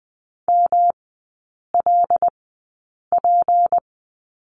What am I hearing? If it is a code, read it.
Morse code, heard as a keyed tone, about 20 wpm, spaced as for 8 wpm, 690 Hz. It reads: MLP
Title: 159 Lesson 26e - Letter g - Three Characters 20wpm